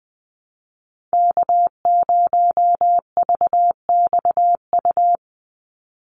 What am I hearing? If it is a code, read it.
Morse code, heard as a keyed tone, about 20 wpm, 700 Hz. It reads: K0VXU
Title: Call Signs Repeated 3x - 20wpm